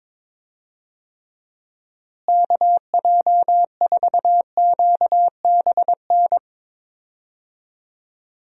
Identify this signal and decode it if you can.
Morse code, heard as a keyed tone, about 22 wpm, 700 Hz. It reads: KJ4QBN
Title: USA Call Signs - 6 Characters - Repeated 2x - 22wpm